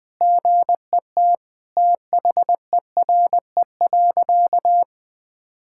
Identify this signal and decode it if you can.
Morse code, heard as a keyed tone, about 21 wpm, 700 Hz. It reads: GET THERE.
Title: Sentences with 2-Word Phrases - 20wpm